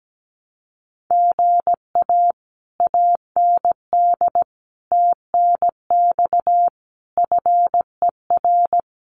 Morse code, heard as a keyed tone, about 17 wpm, 700 Hz. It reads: GA AND TNX FER
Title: Common QSO Phrases Encore 17wpm